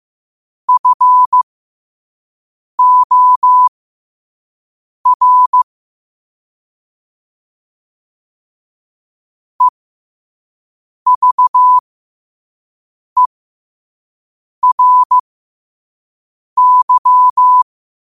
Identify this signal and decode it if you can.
Morse code, heard as a keyed tone, about 15 wpm, spaced as for 5 wpm, 1000 Hz. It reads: FOR EVERY
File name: qotdfiles / 5wpm